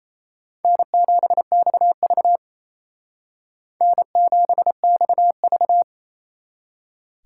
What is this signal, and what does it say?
Morse code, heard as a keyed tone, about 35 wpm, 700 Hz. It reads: N7XV N7XV
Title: Speed Racing - USA Call Signs 28wpm